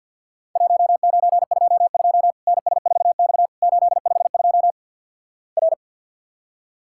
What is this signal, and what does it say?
Morse code, heard as a keyed tone, about 50 wpm, 700 Hz. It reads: 1922 NS4X 852 R
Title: Worked All Europe DX Contest - QTC Exchange - 50wpm